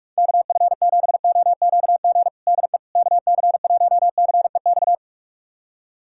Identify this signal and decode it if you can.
Morse code, heard as a keyed tone, about 46 wpm, 690 Hz. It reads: KF7OQG DE KC1CEX
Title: USA Callsign DE Callsign - 6 Characters - 45wpm